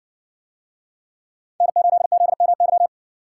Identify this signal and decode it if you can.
Morse code, heard as a keyed tone, about 64 wpm, 700 Hz. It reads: N8ZMY